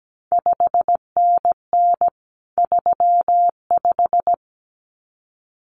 Morse code, heard as a keyed tone, about 17 wpm, 700 Hz. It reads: 5NN 35
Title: CQ WW DX Contest Exchange - 17wpm